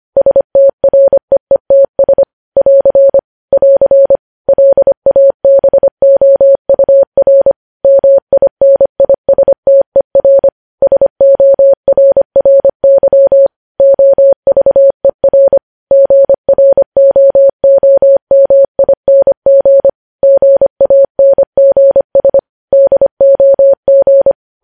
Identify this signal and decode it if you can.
Morse code, heard as a keyed tone, about 25 wpm, 550 Hz. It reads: STREETS <AR> <AR> LABOUR MINISTER SORRY OVER GROOMING GANGS DOG